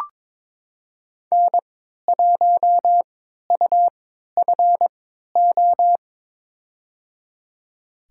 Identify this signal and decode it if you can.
Morse code, heard as a keyed tone, about 22 wpm, spaced as for 13 wpm, 700 Hz. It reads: N1UFO
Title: USA Call Signs - 3x Character Spacing - 22wpm